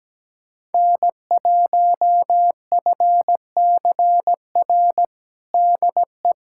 Morse code, heard as a keyed tone, about 17 wpm, 700 Hz. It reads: N1FCR DE
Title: USA Callsign DE Callsign - 5 Characters - 17wpm